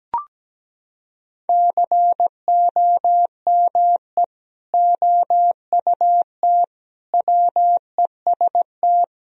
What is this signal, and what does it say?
Morse code, heard as a keyed tone, about 17 wpm, 700 Hz. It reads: COME OUT WEST
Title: Sentences from Top 1000 Words - Review Part 2 - 17wpm